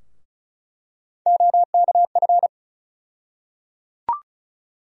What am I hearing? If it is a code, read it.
Morse code, heard as a keyed tone, about 36 wpm, 700 Hz. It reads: OKF
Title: USA Call Sign Postfixes - 3 Letters 35wpm